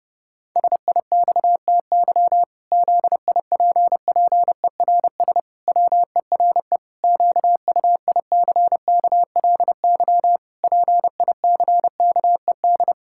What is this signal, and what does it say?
Morse code, heard as a keyed tone, about 30 wpm, 700 Hz. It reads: SIXTY ZIPPERS WERE QUICKLY PICKED
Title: Pangram Challenge v2 - 30wpm